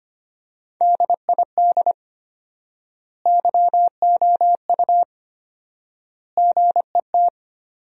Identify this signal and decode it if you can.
Morse code, heard as a keyed tone, about 25 wpm, 700 Hz. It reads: DID YOU GET
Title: 4x Word-Spacing - Sentences from Top 400 Words - 25wpm